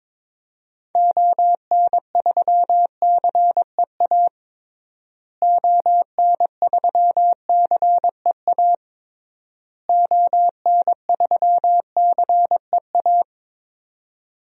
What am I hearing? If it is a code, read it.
Morse code, heard as a keyed tone, about 22 wpm, 700 Hz. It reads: ON3CEA ON3CEA ON3CEA
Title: DXCC-100 Call Signs - 6 Characters - Repeated 3x - 22wpm